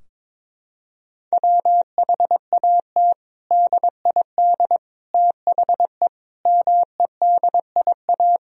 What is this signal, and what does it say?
Morse code, heard as a keyed tone, about 22 wpm, 700 Hz. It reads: WHAT DID THE MEDIA
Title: Sentences from Top 1000 Words - Review Part 4 - 22wpm